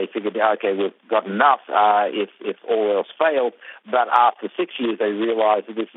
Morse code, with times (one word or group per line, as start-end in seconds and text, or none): none